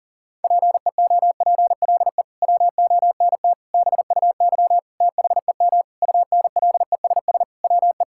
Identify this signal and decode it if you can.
Morse code, heard as a keyed tone, about 40 wpm, 700 Hz. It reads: PEOPLE WONT BUY THEM UNLESS WE
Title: Sentences from Top 1000 Words - Review Part 6 - 40wpm